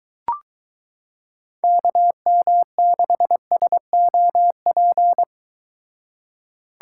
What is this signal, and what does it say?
Morse code, heard as a keyed tone, about 23 wpm, 700 Hz. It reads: KM6SOP